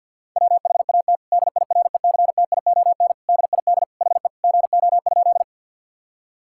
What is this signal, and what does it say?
Morse code, heard as a keyed tone, about 50 wpm, 700 Hz. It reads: WHAT DIRECTION DID HE GO?
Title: Sentences from Top 1000 Words - Review Part 5 - 50wpm